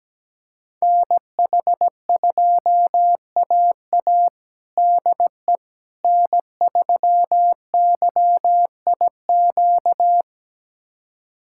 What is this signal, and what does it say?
Morse code, heard as a keyed tone, about 17 wpm, 700 Hz. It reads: NH2AA DE N3YIQ
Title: USA Callsign DE Callsign - 5 Characters - 17wpm